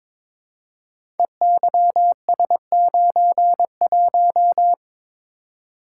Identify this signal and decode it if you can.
Morse code, heard as a keyed tone, about 22 wpm, 700 Hz. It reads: EYS91
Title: Code Groups - 5 Letter:Numbers - 22wpm